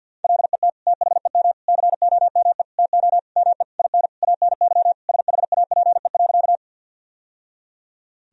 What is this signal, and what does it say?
Morse code, heard as a keyed tone, about 50 wpm, 700 Hz. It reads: LET THEM COME TO ME IN ANY SHAPE.
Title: Sentences from Top 1000 Words - Review Part 3 - 50wpm